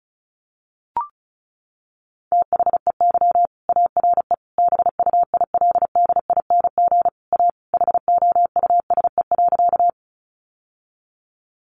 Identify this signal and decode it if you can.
Morse code, heard as a keyed tone, about 34 wpm, 700 Hz. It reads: THEY ARE BUILDING A HOUSE.
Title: Sentences from Top 700 Words 35wpm